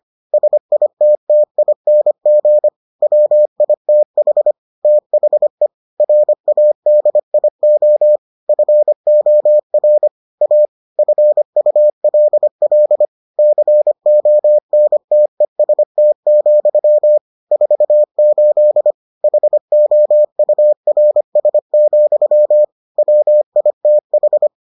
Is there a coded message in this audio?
Yes: Morse code from 0.3 to 24.5 s, heard as a keyed tone, about 25 wpm, 600 Hz. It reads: SITTING WITH THE RADIO FOR A FULL CONTEST, 48 HOURS, WITH